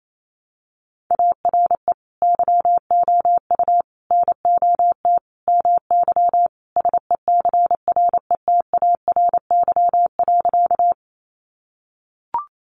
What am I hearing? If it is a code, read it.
Morse code, heard as a keyed tone, about 28 wpm, 700 Hz. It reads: ARE YOU NOT MY SECRETARY.
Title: Sentences from Top 1300 Words 28wpm